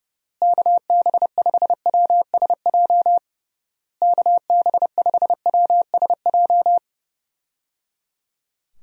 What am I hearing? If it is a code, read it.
Morse code, heard as a keyed tone, about 30 wpm, 700 Hz. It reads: KB5WSJ KB5WSJ
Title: USA Call Signs - 6 Characters - Repeated 3x - 30wpm